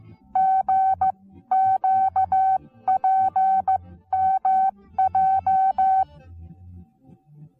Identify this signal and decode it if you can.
Morse code, heard as a keyed tone, about 14 wpm, spaced as for 13 wpm, 760 Hz. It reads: GQPMJ
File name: Onlinelehrgang / morsen / DJ4UF Morsen19